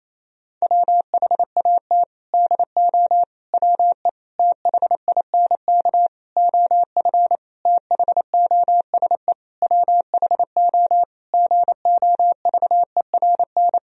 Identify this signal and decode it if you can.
Morse code, heard as a keyed tone, about 27 wpm, 700 Hz. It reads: WHAT DO WE THINK OF THOSE WHO GOVERN
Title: Sentences from Top 1000 Words - Review Part 2 - 28wpm